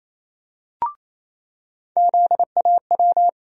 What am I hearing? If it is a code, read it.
Morse code, heard as a keyed tone, about 28 wpm, 700 Hz. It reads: ZAW
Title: USA Call Sign Postfixes - 3 Letters 28wpm